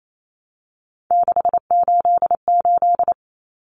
Morse code, heard as a keyed tone, about 28 wpm, 700 Hz. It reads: 688
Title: Numbers - 3-Digit 28wpm